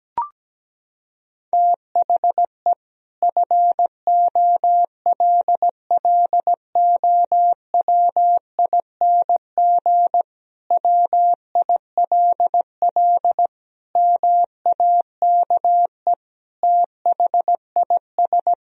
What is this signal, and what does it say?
Morse code, heard as a keyed tone, about 17 wpm, 700 Hz. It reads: THE FOLLOWING WILL MAKE THIS
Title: Sentences from Top 1000 Words - Review Part 4 - 17wpm